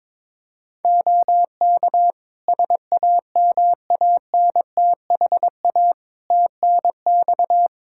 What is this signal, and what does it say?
Morse code, heard as a keyed tone, about 22 wpm, 700 Hz. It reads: OK SAMANTHA TNX
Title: Practice QSO - Exchange 4 - Ending - 22wpm